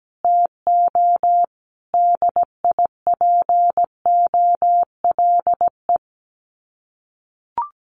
Morse code, heard as a keyed tone, about 17 wpm, 700 Hz. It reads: TO DIPOLE
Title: Common QSO Phrases Encore 17wpm